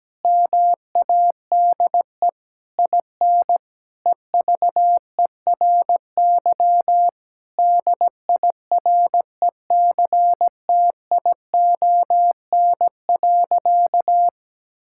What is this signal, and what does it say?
Morse code, heard as a keyed tone, about 17 wpm, 700 Hz. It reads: MADE IN EVERY DIRECTION.
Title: Sentences from Top 1000 Words - Review Part 5 - 17wpm